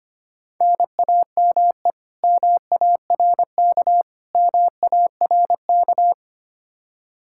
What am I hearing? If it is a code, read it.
Morse code, heard as a keyed tone, about 25 wpm, 700 Hz. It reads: NAME MARK MARK
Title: Practice QSO - Exchange 1 - Beginning - 25wpm